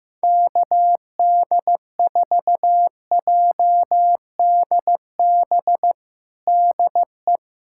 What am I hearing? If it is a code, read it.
Morse code, heard as a keyed tone, about 16 wpm, 700 Hz. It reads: KD4JDB DE
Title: USA Callsign DE Callsign 15wpm